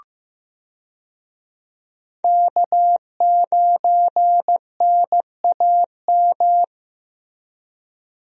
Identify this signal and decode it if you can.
Morse code, heard as a keyed tone, about 15 wpm, 700 Hz. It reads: K9NAM
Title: USA Call Signs - 5 Characters - Repeated 3x - 15wpm